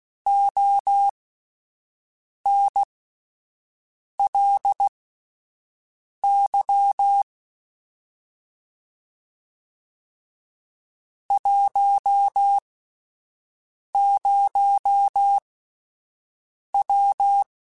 Morse code, heard as a keyed tone, about 16 wpm, spaced as for 5 wpm, 800 Hz. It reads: ONLY 10W
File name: QSOs Spaced